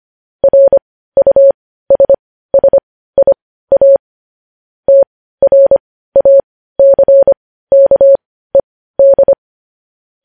Morse code, heard as a keyed tone, about 25 wpm, spaced as for 15 wpm, 550 Hz. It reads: RUSSIA TRACKED
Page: News Headlines in Morse Code at 15 WPM